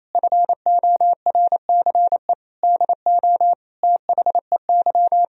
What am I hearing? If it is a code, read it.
Morse code, heard as a keyed tone, about 28 wpm, 700 Hz. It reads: FORCE DO THEY
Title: Sentences from Top 1000 Words - Review Part 3 - 28wpm